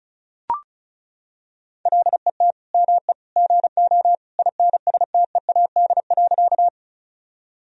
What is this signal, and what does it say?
Morse code, heard as a keyed tone, about 34 wpm, 700 Hz. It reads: LET ME GO INSTEAD.
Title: Sentences from Top 1000 Words - Review Part 4 - 35wpm